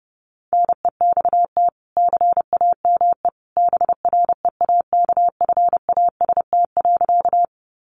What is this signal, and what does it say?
Morse code, heard as a keyed tone, about 30 wpm, 700 Hz. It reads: NEXT CAME BREAKFAST.
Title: Sentences from Top 1500 Words 30wpm